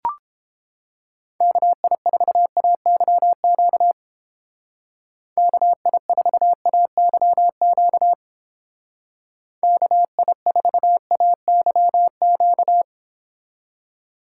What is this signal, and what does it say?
Morse code, heard as a keyed tone, about 31 wpm, 700 Hz. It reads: KI4AYQ KI4AYQ KI4AYQ